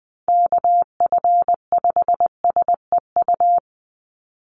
Morse code, heard as a keyed tone, about 19 wpm, 700 Hz. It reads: KF5SEU
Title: USA Call Signs 20wpm